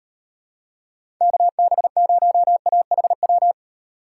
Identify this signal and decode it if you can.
Morse code, heard as a keyed tone, about 36 wpm, 700 Hz. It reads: KB0AHW